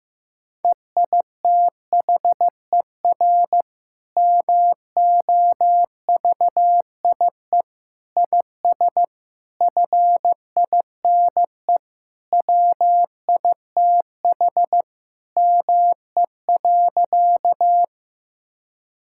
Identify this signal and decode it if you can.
Morse code, heard as a keyed tone, about 14 wpm, 700 Hz. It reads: EITHER MOVIE IS FINE WITH ME.
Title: Sentences from Top 1000 Words - Review Part 5 - 15wpm